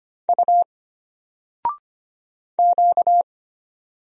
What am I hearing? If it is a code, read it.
Morse code, heard as a keyed tone, about 25 wpm, 700 Hz. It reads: U Q
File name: Single-Letter-Number-25wpm